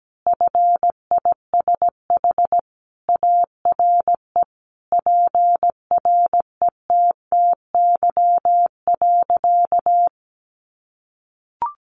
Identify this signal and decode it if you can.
Morse code, heard as a keyed tone, about 17 wpm, 700 Hz. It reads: FISH ARE PRETTY.
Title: Easy Phrases 17wpm